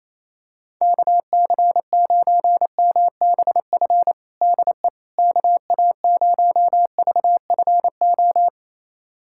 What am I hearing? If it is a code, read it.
Morse code, heard as a keyed tone, about 28 wpm, 700 Hz. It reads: KC9MBF DE KA0VFO
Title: USA Callsign DE Callsign - 6 Characters - 28wpm